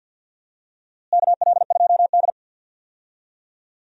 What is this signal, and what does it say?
Morse code, heard as a keyed tone, about 49 wpm, 700 Hz. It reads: KL2D
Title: USA Call Signs - 4 Characters - Repeated 2x - 50wpm